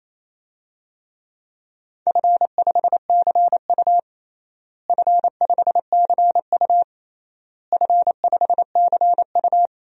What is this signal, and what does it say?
Morse code, heard as a keyed tone, about 27 wpm, 700 Hz. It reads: F5CU F5CU F5CU
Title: DXCC-100 Call Signs - 4 Characters - Repeated 3x - 28wpm